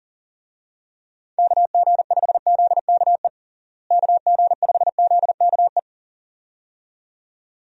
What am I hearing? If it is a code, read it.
Morse code, heard as a keyed tone, about 40 wpm, 700 Hz. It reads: KG5ZKE KG5ZKE
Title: USA Call Signs - 6 Characters - Repeated 2x - 40wpm